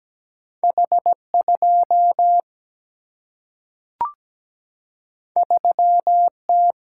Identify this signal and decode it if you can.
Morse code, heard as a keyed tone, about 17 wpm, 700 Hz. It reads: H 2 3 T
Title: Single-Letter Single-Number 17wpm